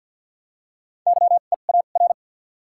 Morse code, heard as a keyed tone, about 52 wpm, spaced as for 37 wpm, 700 Hz. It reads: YEAR
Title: 2x Character Spacing - Top 200 Words - 50wpm